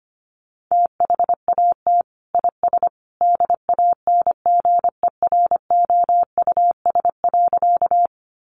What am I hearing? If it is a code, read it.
Morse code, heard as a keyed tone, about 25 wpm, 700 Hz. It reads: THAT IS DANGEROUS.
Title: Sentences from Top 1200 Words 25wpm